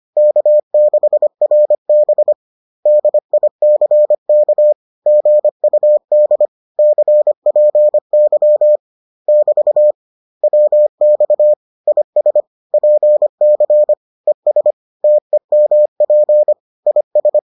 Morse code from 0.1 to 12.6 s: K6RB DICK GUD CPY <BT> WX IS

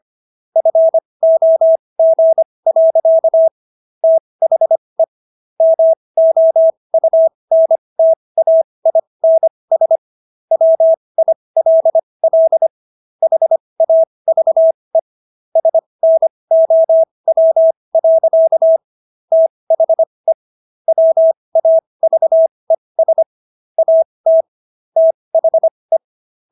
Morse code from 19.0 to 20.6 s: THE

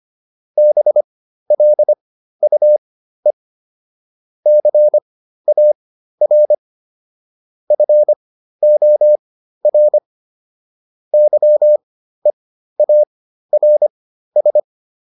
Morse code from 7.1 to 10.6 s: FOR

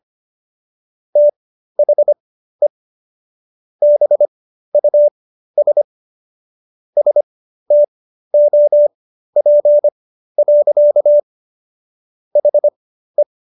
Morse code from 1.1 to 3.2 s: THE

